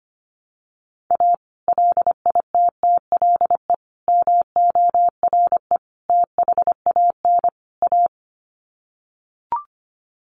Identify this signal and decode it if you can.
Morse code, heard as a keyed tone, about 25 wpm, 700 Hz. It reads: A LITTLE MORE THAN A
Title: Sets of 5 Words Encore2 25wpm